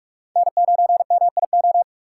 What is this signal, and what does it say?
Morse code, heard as a keyed tone, about 44 wpm, 700 Hz. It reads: N9MIO